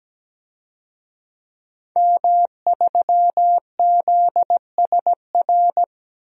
Morse code, heard as a keyed tone, about 17 wpm, 700 Hz. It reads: M3ZSR
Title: DXCC-100 Call Signs - 5 Characters - 17wpm